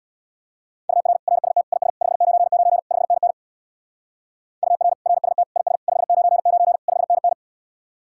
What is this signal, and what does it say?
Morse code, heard as a keyed tone, about 58 wpm, 700 Hz. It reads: UR RST IS 599 5NN UR RST IS 599 5NN
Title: Speed-Racing Common QSO Phrases - 50wpm